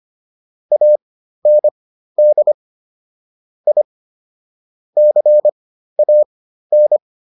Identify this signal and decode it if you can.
Morse code, heard as a keyed tone, about 25 wpm, spaced as for 13 wpm, 600 Hz. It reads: AND I CAN
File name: WD302 13